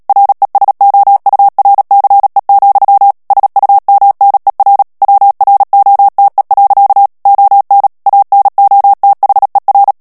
Morse code, heard as a keyed tone, about 37 wpm, 800 Hz. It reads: RESOURCE, SUMNER WROTE. ON ANOTHER
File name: morse mp3 / ANGLAIS texte clair 3 vitesse 35 mots minute